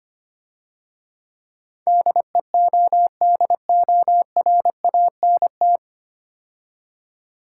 Morse code, heard as a keyed tone, about 27 wpm, 700 Hz. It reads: DEODORANT
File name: Easy Words - Speed-Racing with Hints - 25wpm